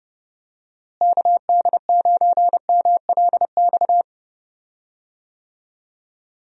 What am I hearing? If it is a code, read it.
Morse code, heard as a keyed tone, about 30 wpm, 700 Hz. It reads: KD9MLX
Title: USA Call Signs - 6 Characters - Repeated 3x - 30wpm